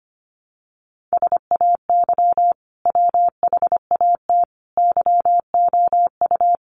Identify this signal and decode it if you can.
Morse code, heard as a keyed tone, about 25 wpm, 700 Hz. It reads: SAY WHAT YOU
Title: Sentences from Top 200 Words 25wpm